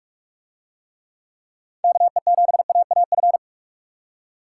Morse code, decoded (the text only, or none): KE7AAF